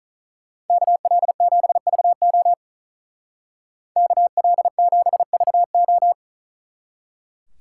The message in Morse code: KL7VO KL7VO

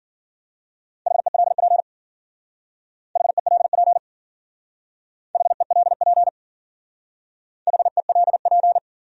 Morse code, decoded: HELP HELP HELP HELP

